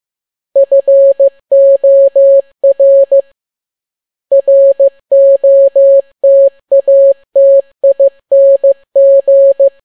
Morse code, heard as a keyed tone, about 15 wpm, 550 Hz. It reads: FOR ROTATING